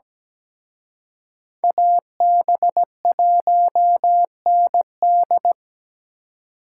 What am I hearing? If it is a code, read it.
Morse code, heard as a keyed tone, about 20 wpm, 700 Hz. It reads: AB1ND
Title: Speed Racing - USA 2x2 Call Signs 17wpm